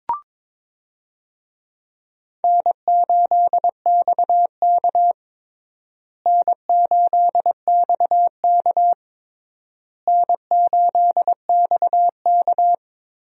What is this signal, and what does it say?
Morse code, heard as a keyed tone, about 21 wpm, 700 Hz. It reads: N8XK N8XK N8XK
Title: USA Call Signs - 4 Characters - Repeated 3x - 22wpm